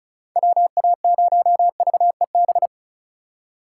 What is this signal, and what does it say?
Morse code, heard as a keyed tone, about 34 wpm, 700 Hz. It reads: WA0VEB
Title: USA Call Signs - 6 Characters - 2x Character Spacing - 35wpm